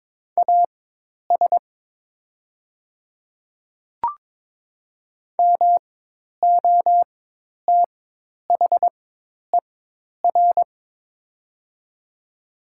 Morse code, heard as a keyed tone, about 22 wpm, spaced as for 10 wpm, 700 Hz. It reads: AS MOTHER
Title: 4x Character Spacing - Top 200 Words - 22wpm